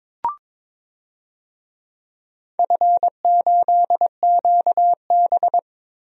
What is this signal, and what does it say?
Morse code, heard as a keyed tone, about 22 wpm, 700 Hz. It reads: F8QB
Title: DXCC-100 Call Signs - 4 Characters - Repeated 3x - 22wpm